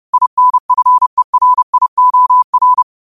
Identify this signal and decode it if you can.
Morse code, heard as a keyed tone, about 30 wpm, 1000 Hz. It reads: INFERIOR